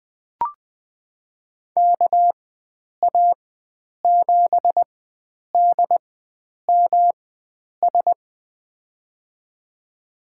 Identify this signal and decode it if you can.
Morse code, heard as a keyed tone, about 21 wpm, spaced as for 9 wpm, 700 Hz. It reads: KA7DMS
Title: USA Call Signs - 6 Characters - 4x Character Spacing - 20wpm